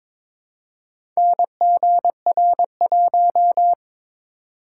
Morse code, heard as a keyed tone, about 22 wpm, 700 Hz. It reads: NGR1
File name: practice / 22wpm / Four Letter-Number Combinations - 22wpm